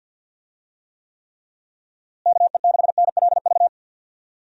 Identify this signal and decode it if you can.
Morse code, heard as a keyed tone, about 52 wpm, 700 Hz. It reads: KE6NLV
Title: USA Call Signs - 6 Characters - Repeated 3x - 50wpm